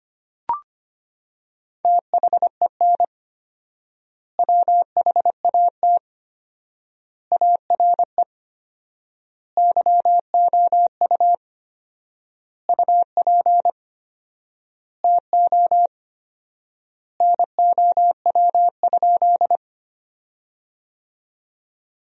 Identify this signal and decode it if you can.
Morse code, heard as a keyed tone, about 25 wpm, 700 Hz. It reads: THEN WHAT ARE YOU UP TO NOW?
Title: 4x Word-Spacing - Sentences from Top 100 Words - 25wpm